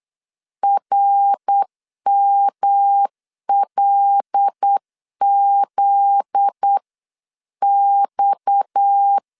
Morse code, heard as a keyed tone, about 8 wpm, 800 Hz. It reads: RMLZX
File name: LEZIONI / 8 YJ45 / 8 YJ4 8wpm 3246